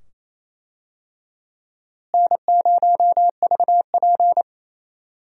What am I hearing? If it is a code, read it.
Morse code, heard as a keyed tone, about 28 wpm, 700 Hz. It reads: N0VP
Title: USA Call Signs - 4 Characters - Repeated 2x - 28wpm